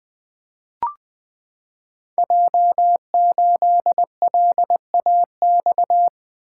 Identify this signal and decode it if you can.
Morse code, heard as a keyed tone, about 20 wpm, 700 Hz. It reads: J8LAX
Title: Code Groups - 5 Letter:Numbers - 20wpm